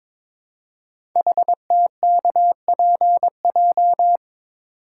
Morse code, heard as a keyed tone, about 23 wpm, 700 Hz. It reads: HTKPJ
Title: Code Groups - 5 Letter:Numbers - 22wpm